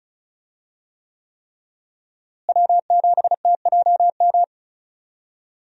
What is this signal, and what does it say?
Morse code, heard as a keyed tone, about 36 wpm, 700 Hz. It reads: W7TJM
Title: USA Call Signs - 5 Characters - Repeated 3x - 35wpm